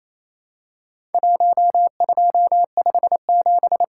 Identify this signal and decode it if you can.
Morse code, heard as a keyed tone, about 28 wpm, 700 Hz. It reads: 1257
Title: Worked All Europe DX Contest - QTC Exchange - 28wpm